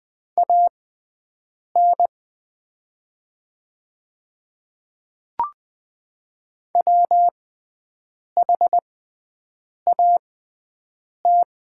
Morse code, heard as a keyed tone, about 20 wpm, spaced as for 7 wpm, 700 Hz. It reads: AN WHAT
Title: 6x Character Spacing - Top 100 Words - 20wpm